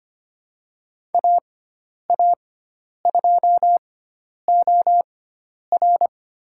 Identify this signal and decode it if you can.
Morse code, heard as a keyed tone, about 25 wpm, spaced as for 10 wpm, 700 Hz. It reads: AA2OR
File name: Lesson 16g - Letter c - Call Signs 25wpm